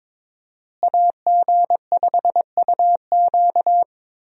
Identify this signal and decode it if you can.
Morse code, heard as a keyed tone, about 22 wpm, 700 Hz. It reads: AG5UQ
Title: USA Call Signs - 5 Characters - Repeated 3x - 22wpm